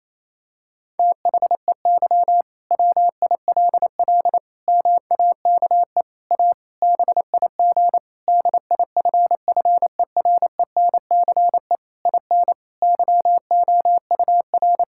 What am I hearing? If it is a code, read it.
Morse code, heard as a keyed tone, about 27 wpm, 700 Hz. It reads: THEY WILL MAKE A BIG DIFFERENCE IN YOUR
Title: Sentences from Top 1000 Words - Review Part 5 - 28wpm